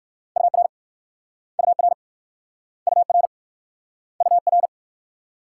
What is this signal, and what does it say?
Morse code, heard as a keyed tone, about 56 wpm, 690 Hz. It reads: UR UR UR UR
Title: Speed-Racing Multi-letter Phonemes - 45wpm